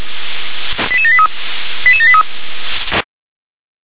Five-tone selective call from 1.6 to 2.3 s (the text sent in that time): 88783